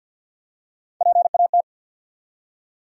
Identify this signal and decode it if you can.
Morse code, heard as a keyed tone, about 46 wpm, 700 Hz. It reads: PAT
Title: USA Common Names - Max 3 Letters - 50wpm